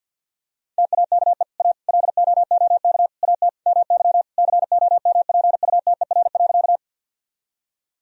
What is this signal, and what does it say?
Morse code, heard as a keyed tone, about 50 wpm, 700 Hz. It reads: TAKE A LOOK AT MY COMPUTER.
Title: Sentences from Top 1000 Words - Review Part 3 - 50wpm